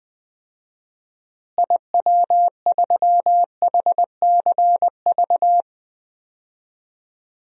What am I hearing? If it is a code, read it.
Morse code, heard as a keyed tone, about 21 wpm, 700 Hz. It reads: IW3HCV
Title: DXCC-100 Call Signs - 6 Characters - Repeated 3x - 20wpm